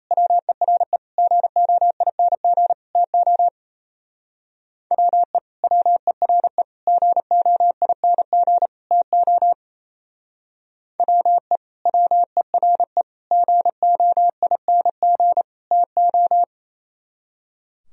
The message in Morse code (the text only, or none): WERE GOING TO WE WERE GOING TO WE WERE GOING TO